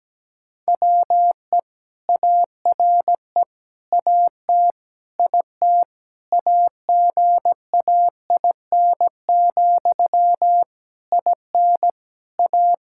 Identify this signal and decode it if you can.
Morse code, heard as a keyed tone, about 17 wpm, 700 Hz. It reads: WE ARE AT IT AGAIN, IN A
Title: Sentences from Top 1000 Words - Review Part 5 - 17wpm